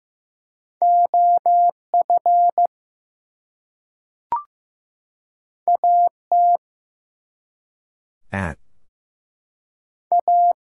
Morse code, heard as a keyed tone, about 15 wpm, 700 Hz. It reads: OF AT A
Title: Two Letter Words - 15wpm